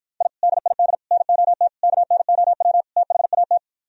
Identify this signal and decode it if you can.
Morse code, heard as a keyed tone, about 53 wpm, 700 Hz. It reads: I DID NOT KNOW THAT